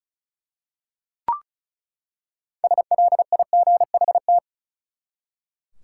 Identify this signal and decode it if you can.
Morse code, heard as a keyed tone, about 34 wpm, 700 Hz. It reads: SLIGHT
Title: Six Letter Words - 35wpm